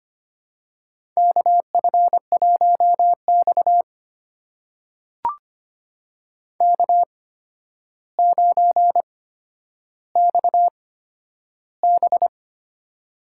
Morse code, heard as a keyed tone, about 25 wpm, spaced as for 11 wpm, 700 Hz. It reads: KF1X K9XB
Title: USA Call Signs - 4 Characters - 8x Character Spacing - 25wpm